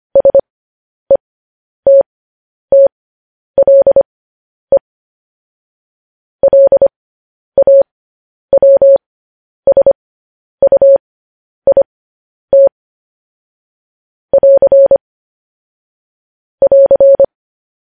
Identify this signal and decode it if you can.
Morse code, heard as a keyed tone, about 25 wpm, spaced as for 10 wpm, 550 Hz. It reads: SETTLE LAWSUIT <AR> <AR>
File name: podcast / News headlines-1x-25x10